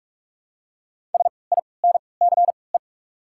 Morse code, heard as a keyed tone, about 44 wpm, 690 Hz. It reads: S I N C E
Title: 3x Character Spacing - Top 300 Words - 45wpm